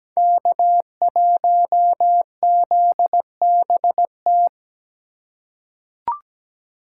Morse code, heard as a keyed tone, about 17 wpm, 700 Hz. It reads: K1ZBT
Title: USA Call Signs - 5 Characters - 2x Character Spacing - 17wpm